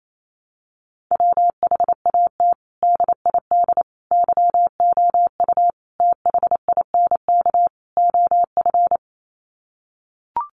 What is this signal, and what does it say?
Morse code, heard as a keyed tone, about 28 wpm, 700 Hz. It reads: WHAT DID YOU THINK OF
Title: Sets of 5 Words Encore2 28wpm